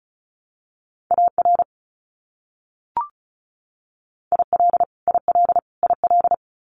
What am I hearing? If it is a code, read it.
Morse code, heard as a keyed tone, about 34 wpm, 700 Hz. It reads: AR IL IL IL
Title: US States Repeated 3x 35wpm